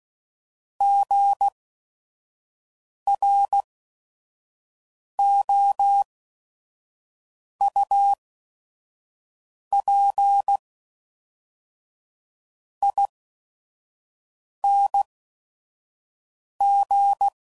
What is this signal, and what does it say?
Morse code, heard as a keyed tone, about 16 wpm, spaced as for 5 wpm, 800 Hz. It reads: GROUPING
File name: Prefixes / By First Letter / Gs